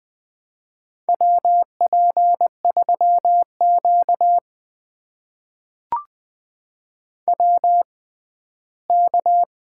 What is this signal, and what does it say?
Morse code, heard as a keyed tone, about 20 wpm, spaced as for 13 wpm, 700 Hz. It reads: WP3Q WK
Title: USA Call Signs - 4 Characters - 6x Character Spacing - 20wpm